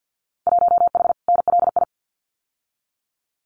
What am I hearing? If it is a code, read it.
Morse code, heard as a keyed tone, about 50 wpm, 700 Hz. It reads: 1H NLI
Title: Winter Field Day Exchanges 50wpm